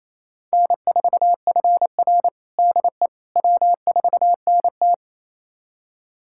Morse code, heard as a keyed tone, about 28 wpm, 700 Hz. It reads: N4FR DE W4NT
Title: USA Callsign DE Callsign - 4 Characters - 28wpm